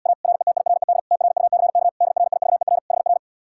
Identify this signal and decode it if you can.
Morse code, heard as a keyed tone, about 79 wpm, 700 Hz. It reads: A RETIRED TRUCK DRIVER HR